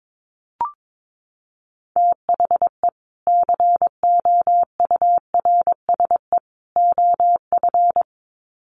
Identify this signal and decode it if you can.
Morse code, heard as a keyed tone, about 22 wpm, 700 Hz. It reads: THE COURSE OF
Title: Sets of 3 Words 22wpm